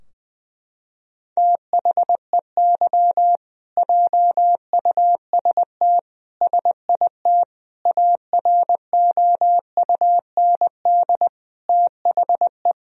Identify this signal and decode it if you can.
Morse code, heard as a keyed tone, about 21 wpm, 700 Hz. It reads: THEY JUST SIT AROUND THE
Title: Sentences from Top 1000 Words - Review Part 3 - 20wpm